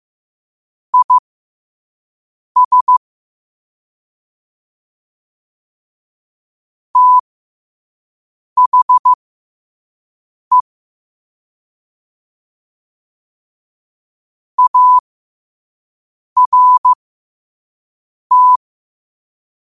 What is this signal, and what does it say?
Morse code, heard as a keyed tone, about 15 wpm, spaced as for 5 wpm, 1000 Hz. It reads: IS THE ART
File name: qotdfiles / 5wpm